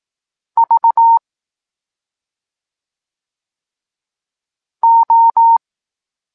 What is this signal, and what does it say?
Morse code, heard as a keyed tone, about 18 wpm, 920 Hz. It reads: V O
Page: Tekst 8 – doskonalenie odbioru poznanych liter